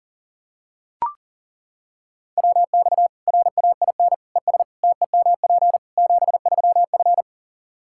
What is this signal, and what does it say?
Morse code, heard as a keyed tone, about 37 wpm, 700 Hz. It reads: WX RAIN ES TEMP 73F
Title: Practice QSO - Exchange 2 - Info - 40wpm